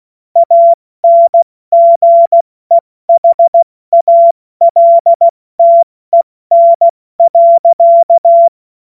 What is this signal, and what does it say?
Morse code, heard as a keyed tone, about 16 wpm, 680 Hz. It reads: ANGEHALTEN.
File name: Morsekurs / Witze / Witz 28. Dezember 2021, 70 F80